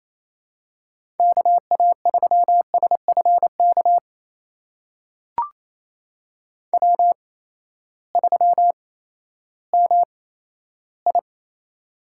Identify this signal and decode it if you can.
Morse code, heard as a keyed tone, about 28 wpm, spaced as for 14 wpm, 700 Hz. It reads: KA3SFK W3MI
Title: USA Call Signs - 8x Character Spacing - 28wpm